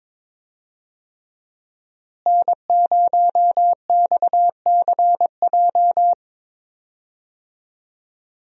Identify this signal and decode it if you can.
Morse code, heard as a keyed tone, about 22 wpm, 700 Hz. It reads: N0XCJ
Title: USA Call Signs - 5 Characters - Repeated 3x - 22wpm